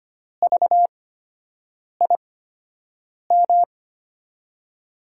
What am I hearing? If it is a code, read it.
Morse code, heard as a keyed tone, about 25 wpm, spaced as for 7 wpm, 700 Hz. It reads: VIM
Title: USA Call Signs - 8x Character Spacing - 25wpm